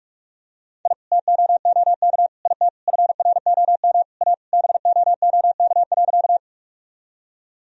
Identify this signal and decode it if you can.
Morse code, heard as a keyed tone, about 46 wpm, 700 Hz. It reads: I TOOK IT FROM A BOOK.
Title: Sentences from Top 1000 Words - Review Part 1 - 45wpm